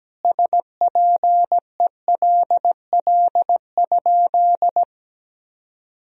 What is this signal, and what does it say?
Morse code, heard as a keyed tone, about 17 wpm, 700 Hz. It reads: SPELL?
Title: Sentences from Top 1000 Words - Review Part 1 - 17wpm